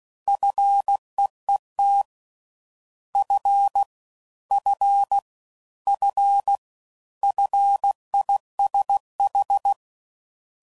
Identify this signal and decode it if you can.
Morse code, heard as a keyed tone, about 16 wpm, 800 Hz. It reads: FEET F F F FISH